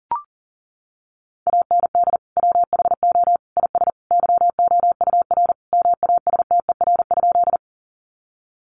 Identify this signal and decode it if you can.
Morse code, heard as a keyed tone, about 40 wpm, 700 Hz. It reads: AND WHO IS YOUR MASTER?
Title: Sentences from Top 1300 Words 40wpm